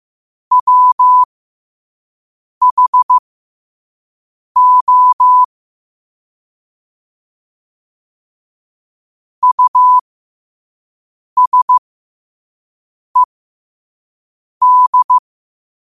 Morse code, heard as a keyed tone, about 15 wpm, spaced as for 5 wpm, 1000 Hz. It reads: WHO USED
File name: qotdfiles / 5wpm